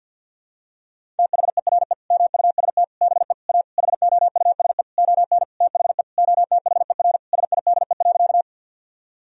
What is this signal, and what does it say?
Morse code, heard as a keyed tone, about 46 wpm, 700 Hz. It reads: THERE MUST BE A HOUSE ON THE OTHER SIDE.
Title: Sentences from Top 1000 Words - Review Part 1 - 50wpm